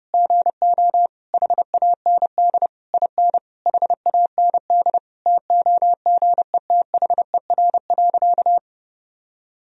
Morse code, heard as a keyed tone, about 32 wpm, 700 Hz. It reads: GO HAND IN HAND TOGETHER.
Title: Sentences from Top 1000 Words - Review Part 4 - 30wpm